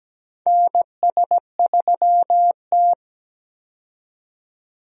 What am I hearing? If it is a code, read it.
Morse code, heard as a keyed tone, about 17 wpm, 700 Hz. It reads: NS3T
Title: USA Call Signs - 4 Characters - 17wpm